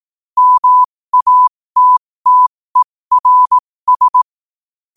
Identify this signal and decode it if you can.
Morse code, heard as a keyed tone, about 18 wpm, spaced as for 15 wpm, 1000 Hz. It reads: MATTERS
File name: 2024-11-18-15wpm